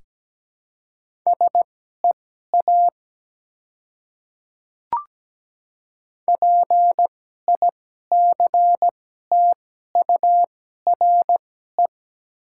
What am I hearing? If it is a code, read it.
Morse code, heard as a keyed tone, about 17 wpm, spaced as for 12 wpm, 700 Hz. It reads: SEA PICTURE
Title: 2x Character Spacing - Top 300 Words - 17wpm